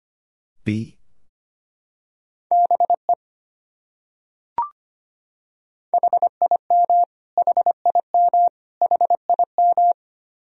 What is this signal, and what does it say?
Morse code, heard as a keyed tone, about 25 wpm, 700 Hz. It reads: BE HIM HIM HIM
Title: Top 100 Words Repeated 25wpm